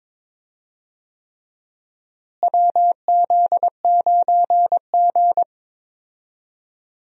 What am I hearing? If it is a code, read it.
Morse code, heard as a keyed tone, about 22 wpm, 700 Hz. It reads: WZ9G
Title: USA Call Signs - 4 Characters - Repeated 3x - 22wpm